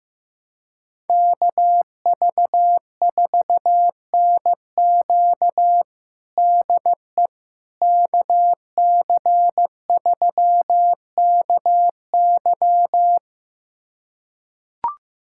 Morse code, heard as a keyed tone, about 16 wpm, 700 Hz. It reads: KV4NQ DE KC3KY
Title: USA Callsign DE Callsign 15wpm